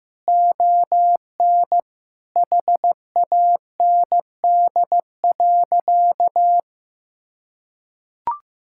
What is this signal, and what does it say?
Morse code, heard as a keyed tone, about 15 wpm, 700 Hz. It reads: ON HAND.
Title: Sentences from Top 1000 Words - Review Part 5 - 15wpm